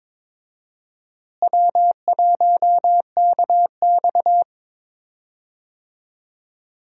Morse code, heard as a keyed tone, about 22 wpm, 700 Hz. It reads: W1KX
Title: USA Call Signs - 4 Characters - Repeated 2x - 22wpm